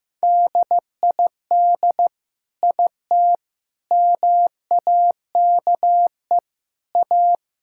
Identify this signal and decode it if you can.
Morse code, heard as a keyed tone, about 15 wpm, 700 Hz. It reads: DID IT MAKE A
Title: Sentences from Top 1000 Words - Review Part 5 - 15wpm